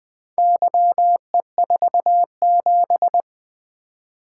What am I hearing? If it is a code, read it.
Morse code, heard as a keyed tone, about 20 wpm, 700 Hz. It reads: YE47
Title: Four Letter-Number Combinations - 20wpm